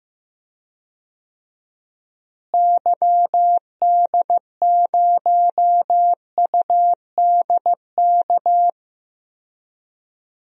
Morse code, heard as a keyed tone, about 15 wpm, 700 Hz. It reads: YD0UDK
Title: DXCC-100 Call Signs - 6 Characters - Repeated 3x - 15wpm